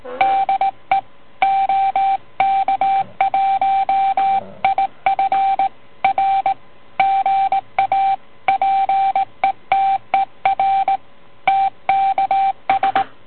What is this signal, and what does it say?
Morse code, heard as a keyed tone, about 19 wpm, 780 Hz, with radio noise. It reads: DE OK1IF R GA PETER TKS